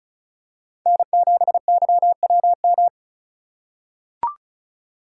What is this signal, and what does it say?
Morse code, heard as a keyed tone, about 36 wpm, 700 Hz. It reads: N7YWM
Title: USA Call Signs - 5 Characters - 3x Character Spacing - 35wpm